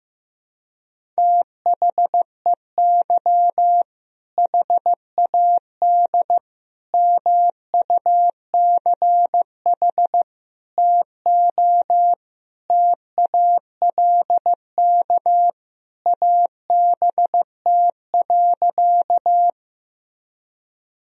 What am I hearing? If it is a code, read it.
Morse code, heard as a keyed tone, about 15 wpm, 700 Hz. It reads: THEY HAD MUCH TO TALK ABT.
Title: Sentences from Top 400 Words and Abbreviations 15wpm